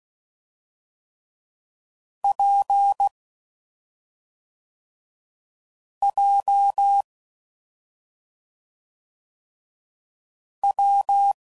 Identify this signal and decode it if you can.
Morse code, heard as a keyed tone, about 16 wpm, 800 Hz. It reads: P J W
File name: Combinations / Difficult Combos / WPJ WPJ1